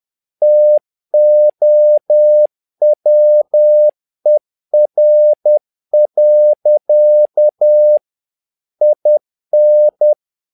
Morse code, heard as a keyed tone, about 10 wpm, 600 Hz. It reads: TOWER. IN
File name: learning center / tower trouble / tower trouble 10wpm